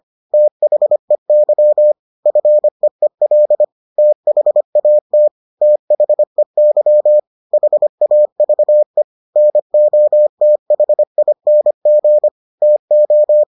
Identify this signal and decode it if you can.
Morse code, heard as a keyed tone, about 25 wpm, 600 Hz. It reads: THEY FEEL THAT THEY HAVE NOTHING TO